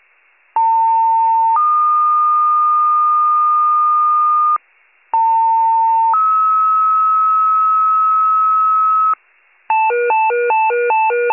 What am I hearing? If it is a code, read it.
Two-tone page, heard as a one-second tone, then a three-second tone, about 900 then 1220 Hz, about 900 then 1290 Hz. It is heard from 0.6 to 9.1 s.